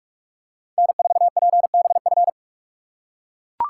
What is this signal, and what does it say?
Morse code, heard as a keyed tone, about 44 wpm, 700 Hz. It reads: N4PBF